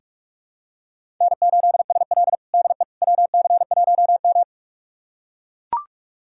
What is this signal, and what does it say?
Morse code, heard as a keyed tone, about 42 wpm, 700 Hz. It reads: N8SL DE WC1M
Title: USA Callsign DE Callsign - 4 Characters - 45wpm